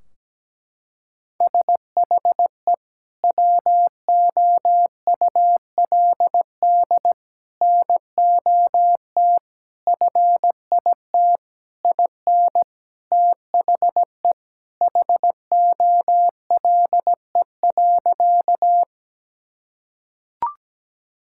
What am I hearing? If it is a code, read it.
Morse code, heard as a keyed tone, about 17 wpm, 700 Hz. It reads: SHE WOULD NOT FIT IN THE HOLE.
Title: Sentences from Top 1600 Words 17wpm